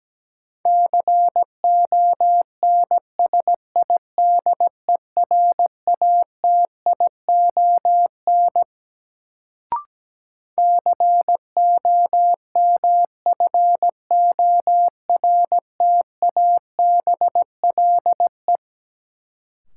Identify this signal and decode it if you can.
Morse code, heard as a keyed tone, about 17 wpm, 700 Hz. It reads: CONSIDERATION COMFORTABLE
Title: Sentences from Top 1500 Words - New Word Drill Alt - 17wpm